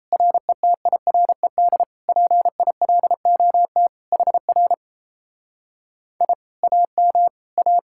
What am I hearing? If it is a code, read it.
Morse code, heard as a keyed tone, about 37 wpm, 700 Hz. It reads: RETIRED PILOT HR I AM A